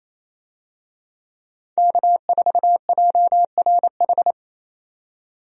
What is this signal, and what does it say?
Morse code, heard as a keyed tone, about 28 wpm, 700 Hz. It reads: K4JRH
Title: USA Call Signs - 5 Characters - 28wpm